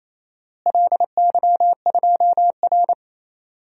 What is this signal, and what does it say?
Morse code, heard as a keyed tone, about 28 wpm, 700 Hz. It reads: LY2R
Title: DXCC-100 Call Signs - 4 Characters - Repeated 2x - 28wpm